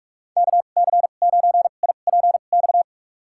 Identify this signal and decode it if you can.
Morse code, heard as a keyed tone, about 41 wpm, spaced as for 31 wpm, 700 Hz. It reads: KC9IPX
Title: USA Call Signs - 6 Characters - 2x Character Spacing - 45wpm